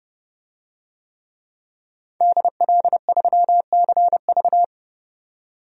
Morse code, heard as a keyed tone, about 30 wpm, 700 Hz. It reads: DL3CV
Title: DXCC-100 Call Signs - 5 Characters - 30wpm